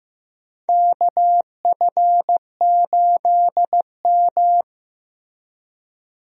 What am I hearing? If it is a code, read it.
Morse code, heard as a keyed tone, about 15 wpm, 700 Hz. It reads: KF8M
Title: USA Call Signs - 4 Characters - Repeated 2x - 15wpm